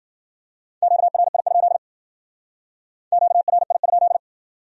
Morse code, heard as a keyed tone, about 52 wpm, 700 Hz. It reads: QRI? QRI?